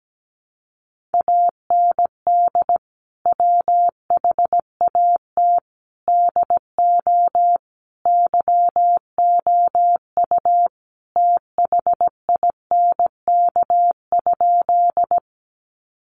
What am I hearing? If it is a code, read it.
Morse code, heard as a keyed tone, about 17 wpm, 700 Hz. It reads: AND WHAT DO YOU THINK?
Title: Sentences from Top 200 Words 17wpm